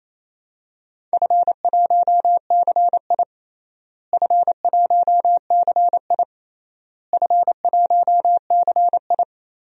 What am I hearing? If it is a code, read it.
Morse code, heard as a keyed tone, about 28 wpm, 700 Hz. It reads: F1CI F1CI F1CI
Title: DXCC-100 Call Signs - 4 Characters - Repeated 3x - 28wpm